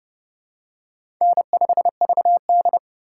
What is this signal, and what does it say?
Morse code, heard as a keyed tone, about 30 wpm, 700 Hz. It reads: N5VD
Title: USA Call Signs - 4x Character Spacing - 30wpm